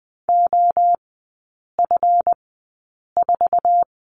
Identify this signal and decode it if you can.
Morse code, heard as a keyed tone, about 20 wpm, spaced as for 8 wpm, 700 Hz. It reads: OF4
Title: 138 Lesson 23e - Letter f - Three Characters 20wpm